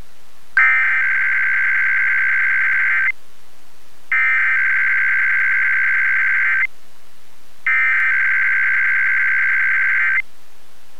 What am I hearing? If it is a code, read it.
Emergency Alert System (SAME) tones (header bursts), heard as harsh buzzing bursts. It is heard from 0.6 to 10.2 s.